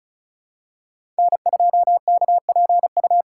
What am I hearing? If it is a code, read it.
Morse code, heard as a keyed tone, about 36 wpm, 700 Hz. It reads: N2KPU
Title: USA Call Signs - 5 Characters - 2x Character Spacing - 35wpm